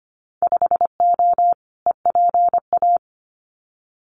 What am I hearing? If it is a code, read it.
Morse code, heard as a keyed tone, about 25 wpm, 700 Hz. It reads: 5O EPA
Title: Winter Field Day Exchanges Encore 25wpm